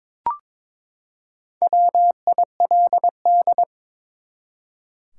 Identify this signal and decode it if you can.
Morse code, heard as a keyed tone, about 22 wpm, 700 Hz. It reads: WILD
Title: Four Letter Words - 22wpm